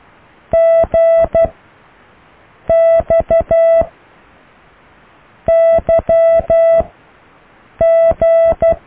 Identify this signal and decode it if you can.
Morse code, heard as a keyed tone, about 12 wpm, 660 Hz. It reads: G X Y G